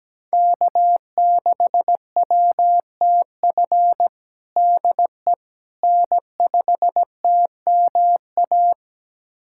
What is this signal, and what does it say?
Morse code, heard as a keyed tone, about 17 wpm, 700 Hz. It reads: K6WTF DE N5TMA
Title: USA Callsign DE Callsign - 5 Characters - 17wpm